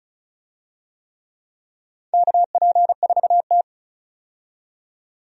Morse code, heard as a keyed tone, about 34 wpm, 700 Hz. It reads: KP4T
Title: DXCC-100 Call Signs - 4 Characters - 35wpm